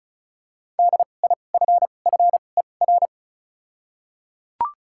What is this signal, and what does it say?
Morse code, heard as a keyed tone, about 34 wpm, spaced as for 25 wpm, 700 Hz. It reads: DIFFER
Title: 2x Character Spacing - Top 500 Words - 35wpm